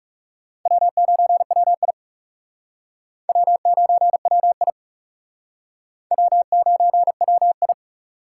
Morse code, heard as a keyed tone, about 40 wpm, 700 Hz. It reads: W9WI W9WI W9WI